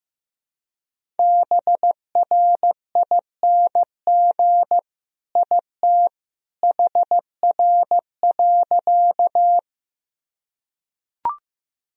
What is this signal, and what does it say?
Morse code, heard as a keyed tone, about 16 wpm, 700 Hz. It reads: BRING IT HR.
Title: Sentences from Top 500 Words and Abbreviations 15wpm